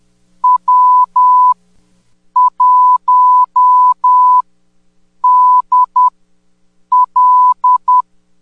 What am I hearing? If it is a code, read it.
Morse code, heard as a keyed tone, about 18 wpm, spaced as for 8 wpm, 1000 Hz. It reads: W1DL
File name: 10m-beacon